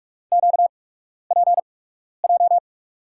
Morse code, heard as a keyed tone, about 48 wpm, spaced as for 12 wpm, 700 Hz. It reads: QPJ
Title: USA Call Signs - 6 Characters - 8x Character Spacing - 45wpm